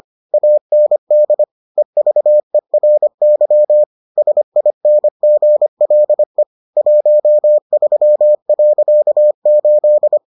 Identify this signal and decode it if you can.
Morse code, heard as a keyed tone, about 25 wpm, 600 Hz. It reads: AND EVERY SINGLE 13.8